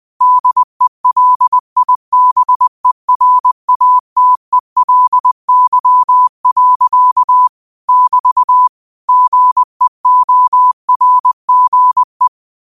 Morse code, heard as a keyed tone, about 20 wpm, 1000 Hz. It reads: DELIBERATELY. <BT> GEORGE